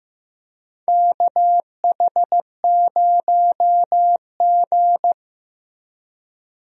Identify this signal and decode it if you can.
Morse code, heard as a keyed tone, about 15 wpm, 700 Hz. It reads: KH0G
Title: USA Call Signs - 4 Characters - Repeated 3x - 15wpm